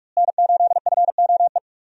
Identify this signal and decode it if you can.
Morse code, heard as a keyed tone, about 46 wpm, 700 Hz. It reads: N8FOE
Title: USA Call Signs - 5 Characters - Repeated 2x - 45wpm